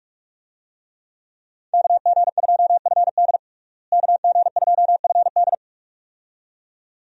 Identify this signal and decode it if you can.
Morse code, heard as a keyed tone, about 46 wpm, 700 Hz. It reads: KG2FD KG2FD
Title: USA Call Signs - 5 Characters - Repeated 2x - 45wpm